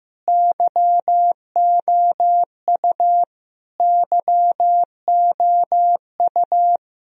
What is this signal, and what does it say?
Morse code, heard as a keyed tone, about 15 wpm, 700 Hz. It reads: YOU YOU
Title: Top 100 Words Repeated 15wpm